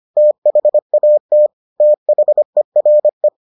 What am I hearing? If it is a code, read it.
Morse code, heard as a keyed tone, about 25 wpm, 600 Hz. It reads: THAT THERE